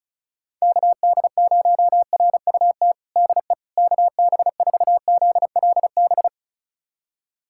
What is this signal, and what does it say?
Morse code, heard as a keyed tone, about 36 wpm, 690 Hz. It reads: KD0RUT DE KB4ZLB
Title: USA Callsign DE Callsign - 6 Characters - 35wpm